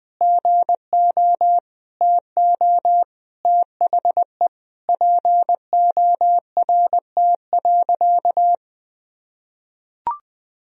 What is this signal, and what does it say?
Morse code, heard as a keyed tone, about 20 wpm, 700 Hz. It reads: GO TO THE PORT.
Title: Sentences from Top 1000 Words - Review Part 1 - 20wpm